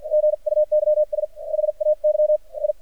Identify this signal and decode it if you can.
Morse code, heard as a keyed tone, about 50 wpm, spaced as for 43 wpm, 610 Hz. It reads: GUYS HAY I
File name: archived / CRSnet